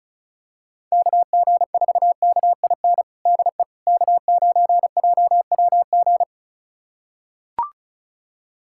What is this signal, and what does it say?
Morse code, heard as a keyed tone, about 36 wpm, 690 Hz. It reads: KG4KIN DE K9JWG
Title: USA Callsign DE Callsign 35wpm